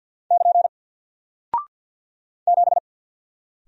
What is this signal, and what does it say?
Morse code, heard as a keyed tone, about 48 wpm, 700 Hz. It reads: <KN> 7
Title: Single Character (Letter:Number:Punctuation:Prosign) Alt 50wpm